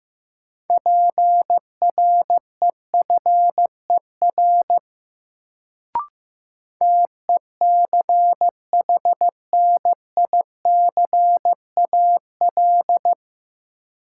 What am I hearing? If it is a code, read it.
Morse code, heard as a keyed tone, about 15 wpm, 700 Hz. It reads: PREFER TECHNICAL
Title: Sentences from Top 1100 Words - New Word Drill Alt - 15wpm